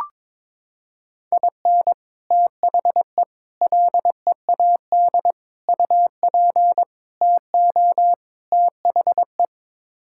Morse code, heard as a keyed tone, about 21 wpm, 700 Hz. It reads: IN THE LEAD UP TO THE
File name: practice / 22wpm / Sets of 6 Words - 22wpm